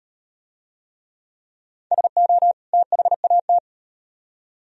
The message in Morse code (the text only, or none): SO THAT